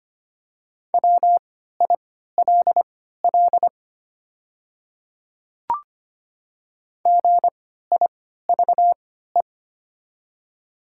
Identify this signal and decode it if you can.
Morse code, heard as a keyed tone, about 25 wpm, spaced as for 14 wpm, 700 Hz. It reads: WILL GIVE
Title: 3x Character Spacing - Top 200 Words - 25wpm